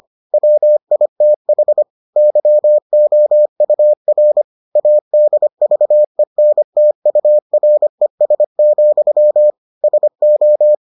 Morse code, heard as a keyed tone, about 25 wpm, 600 Hz. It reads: WITH YOUR ADVENTURES, SO